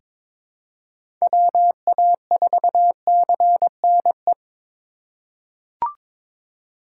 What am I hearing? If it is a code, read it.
Morse code, heard as a keyed tone, about 23 wpm, 700 Hz. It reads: WA4CNE
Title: USA Call Signs - 6 Characters - 3x Character Spacing - 22wpm